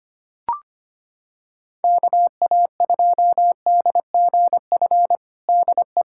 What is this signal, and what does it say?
Morse code, heard as a keyed tone, about 25 wpm, 700 Hz. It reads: KA2DGF DE
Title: USA Callsign DE Callsign - 6 Characters - 25wpm